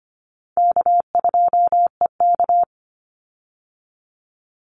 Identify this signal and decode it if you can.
Morse code, heard as a keyed tone, about 25 wpm, 700 Hz. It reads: K2EK
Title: Easy Call Signs Repeated 2x - 25wpm